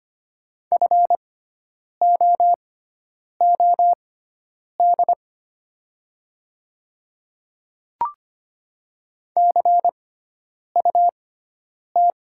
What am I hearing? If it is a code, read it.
Morse code, heard as a keyed tone, about 25 wpm, spaced as for 9 wpm, 700 Hz. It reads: FOOD CUT
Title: 6x Character Spacing - Top 500 Words - 25wpm